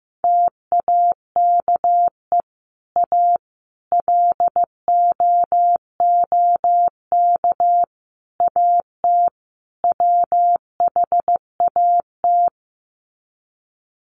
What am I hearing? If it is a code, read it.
Morse code, heard as a keyed tone, about 14 wpm, 700 Hz. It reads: TAKE A LOOK AT WHAT
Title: Sets of 5 Words Encore2 15wpm